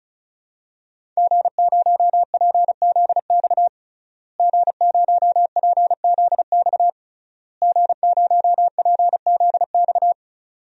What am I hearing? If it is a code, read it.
Morse code, heard as a keyed tone, about 36 wpm, 700 Hz. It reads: G0PZX G0PZX G0PZX
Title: DXCC-100 Call Signs - 5 Characters - Repeated 3x - 35wpm